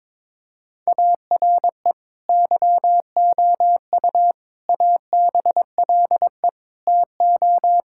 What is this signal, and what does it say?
Morse code, heard as a keyed tone, about 22 wpm, 700 Hz. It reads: ARE YOU ABLE TO
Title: Sentences from Top 1000 Words - Review Part 5 - 22wpm